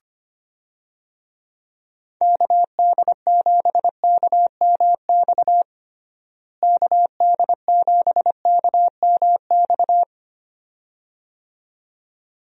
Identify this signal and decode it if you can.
Morse code, heard as a keyed tone, about 25 wpm, 700 Hz. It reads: KD7KMX KD7KMX
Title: USA Call Signs - 6 Characters - Repeated 2x - 25wpm